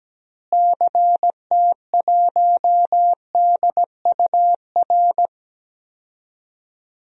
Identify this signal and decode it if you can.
Morse code, heard as a keyed tone, about 17 wpm, 700 Hz. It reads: CT1DUR
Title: DXCC-100 Call Signs - 6 Characters - Repeated 3x - 17wpm